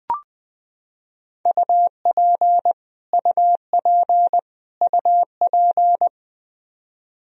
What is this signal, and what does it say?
Morse code, heard as a keyed tone, about 20 wpm, 700 Hz. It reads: UP UP UP
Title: Top 100 Words Repeated 20wpm